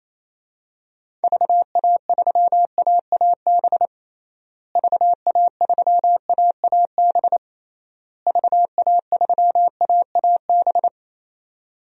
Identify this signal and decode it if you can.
Morse code, heard as a keyed tone, about 28 wpm, 700 Hz. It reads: VA3AAB VA3AAB VA3AAB
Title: DXCC-100 Call Signs - 6 Characters - Repeated 3x - 28wpm